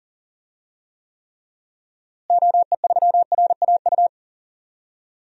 Morse code, heard as a keyed tone, about 38 wpm, 700 Hz. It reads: OE3RAU
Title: DXCC-100 Call Signs - 6 Characters - Repeated 3x - 40wpm